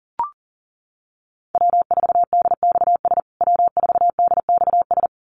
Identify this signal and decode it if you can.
Morse code, heard as a keyed tone, about 40 wpm, 700 Hz. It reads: W4DXS W4DXS
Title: USA Call Signs Repeated 2x 40wpm